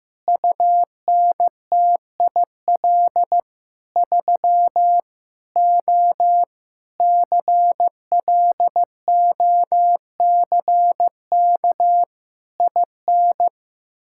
Morse code, heard as a keyed tone, about 16 wpm, 700 Hz. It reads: UNTIL 3 O CLOCK IN
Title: Pangram Challenge v2 - 15wpm